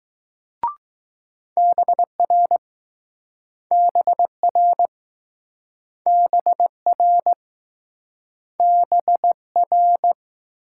Morse code, heard as a keyed tone, about 20 wpm, 700 Hz. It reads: BR BR BR BR